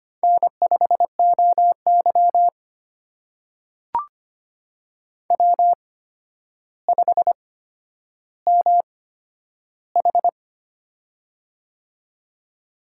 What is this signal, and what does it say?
Morse code, heard as a keyed tone, about 25 wpm, spaced as for 11 wpm, 700 Hz. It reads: N5OY W5MH
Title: USA Call Signs - 4 Characters - 8x Character Spacing - 25wpm